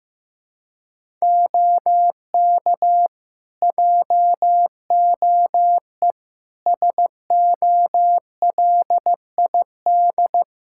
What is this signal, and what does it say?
Morse code, heard as a keyed tone, about 15 wpm, 700 Hz. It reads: OK JOE SOLID
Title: Practice QSO - Exchange 3 - More Info - 15wpm